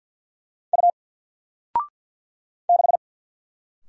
Morse code, heard as a keyed tone, about 50 wpm, 700 Hz. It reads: U 6
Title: Single Letter-Number 50wpm